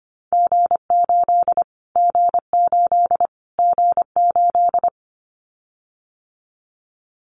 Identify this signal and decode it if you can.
Morse code, heard as a keyed tone, about 25 wpm, 700 Hz. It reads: G8 G8 G8
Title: Callsign Prefixes for Top 25 DXCC - Repeated 3x - 25wpm